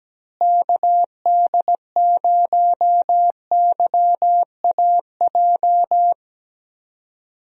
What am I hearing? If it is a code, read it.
Morse code, heard as a keyed tone, about 17 wpm, 700 Hz. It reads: KD0YAJ
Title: USA Call Signs - 6 Characters - Repeated 3x - 17wpm